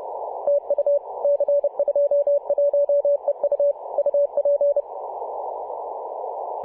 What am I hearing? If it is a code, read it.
Morse code, heard as a keyed tone, about 30 wpm, 570 Hz. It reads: TU C21EU UP
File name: posts / mar-25-2226z-c21eu-21-020mhz-cw